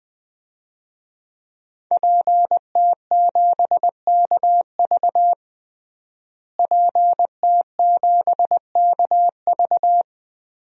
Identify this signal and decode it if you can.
Morse code, heard as a keyed tone, about 21 wpm, 700 Hz. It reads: PT7KV PT7KV
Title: DXCC-100 Call Signs - 5 Characters - Repeated 3x - 20wpm